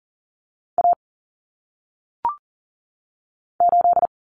Morse code, heard as a keyed tone, about 40 wpm, 700 Hz. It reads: A 8
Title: Common Cut Numbers - 1-Digit 40wpm